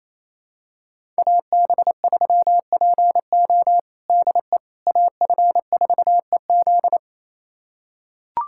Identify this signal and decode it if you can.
Morse code, heard as a keyed tone, about 28 wpm, 700 Hz. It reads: AB3PO DE AF4EZ
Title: USA Callsign DE Callsign - 5 Characters - 28wpm